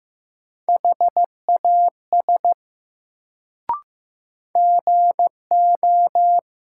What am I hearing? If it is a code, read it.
Morse code, heard as a keyed tone, about 15 wpm, 700 Hz. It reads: HAS GO
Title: Sentences from Top 100 Words - New Word Drill - 15wpm